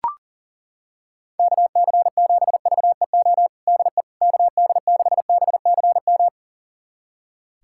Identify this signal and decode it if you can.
Morse code, heard as a keyed tone, about 40 wpm, 700 Hz. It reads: KC7VEO DE KD6BCM
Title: USA Callsign DE Callsign - 6 Characters - 40wpm